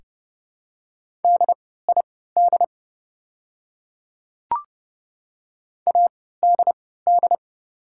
Morse code, heard as a keyed tone, about 32 wpm, spaced as for 17 wpm, 690 Hz. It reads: DID ADD
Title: 3x Character Spacing - Top 200 Words - 30wpm